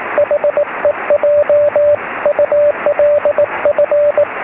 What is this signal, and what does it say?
Morse code, heard as a keyed tone, about 19 wpm, 590 Hz, with radio noise. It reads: HEJ ULF
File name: sm7lcb / 12-sm7ecm